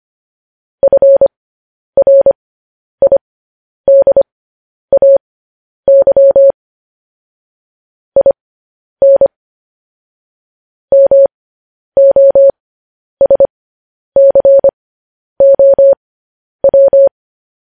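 Morse code, heard as a keyed tone, about 25 wpm, spaced as for 10 wpm, 550 Hz. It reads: FRIDAY IN MOSCOW
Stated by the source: Morse code news headlines at 10.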